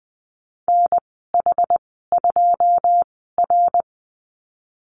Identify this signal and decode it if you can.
Morse code, heard as a keyed tone, about 20 wpm, spaced as for 15 wpm, 700 Hz. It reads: NH2R
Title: 122 Lesson 20g - Number 3 - Call Signs 20wpm